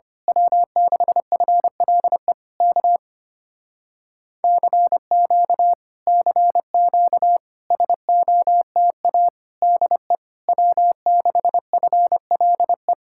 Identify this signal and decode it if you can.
Morse code, heard as a keyed tone, about 24 wpm, 700 Hz. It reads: W6FLE K CQ CQ SOTA DE W6FLE
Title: Speed-Racing Common QSO Phrases Encore - 25wpm